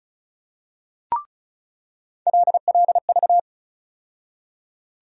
Morse code, heard as a keyed tone, about 36 wpm, 700 Hz. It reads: LLV
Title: USA Call Sign Postfixes - 3 Letters 35wpm